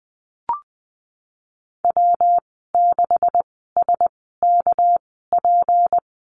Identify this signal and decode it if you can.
Morse code, heard as a keyed tone, about 19 wpm, spaced as for 14 wpm, 700 Hz. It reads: W6SKP
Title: 219 Lesson 35g - Number 0 - Call Signs 20wpm